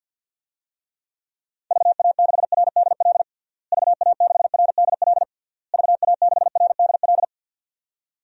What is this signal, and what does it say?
Morse code, heard as a keyed tone, about 52 wpm, 700 Hz. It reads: VA6RDL VA6RDL VA6RDL
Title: DXCC-100 Call Signs - 6 Characters - Repeated 3x - 50wpm